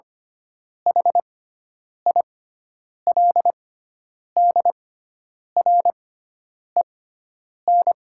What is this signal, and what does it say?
Morse code, heard as a keyed tone, about 25 wpm, spaced as for 9 wpm, 700 Hz. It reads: HILDREN
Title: 6x Character Spacing - Top 500 Words - 25wpm